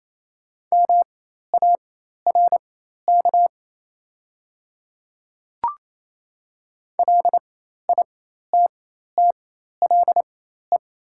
Morse code, heard as a keyed tone, about 28 wpm, spaced as for 13 wpm, 700 Hz. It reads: MARK LITTLE
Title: 4x Character Spacing - Top 300 Words - 28wpm